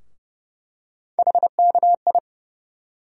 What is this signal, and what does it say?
Morse code, heard as a keyed tone, about 30 wpm, 700 Hz. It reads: HKI
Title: USA Call Sign Postfixes - 3 Letters 30wpm